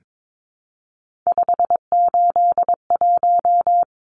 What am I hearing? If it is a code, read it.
Morse code, heard as a keyed tone, about 22 wpm, 700 Hz. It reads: 581
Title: Numbers - 3-Digit 22wpm